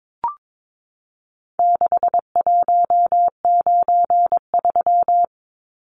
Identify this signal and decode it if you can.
Morse code, heard as a keyed tone, about 22 wpm, 700 Hz. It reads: 6193
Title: Numbers - 4-Digit 22wpm